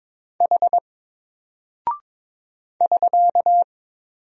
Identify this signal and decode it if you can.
Morse code, heard as a keyed tone, about 22 wpm, 700 Hz. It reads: H <SK>
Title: Single Character (Letter:Number:Punctuation:Prosign) Alt 22wpm